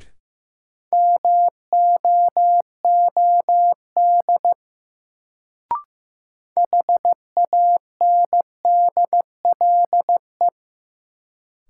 Morse code, heard as a keyed tone, about 15 wpm, 700 Hz. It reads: MOOD HANDLE
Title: Sentences from Top 1100 Words - New Word Drill - 15wpm